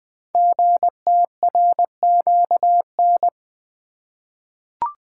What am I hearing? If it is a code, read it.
Morse code, heard as a keyed tone, about 20 wpm, 700 Hz. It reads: GTRQN
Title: Code Groups - 5 Letters - 20wpm